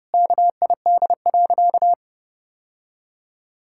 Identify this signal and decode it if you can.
Morse code, heard as a keyed tone, about 32 wpm, 700 Hz. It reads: KID.
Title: Sentences from Top 1600 Words 30wpm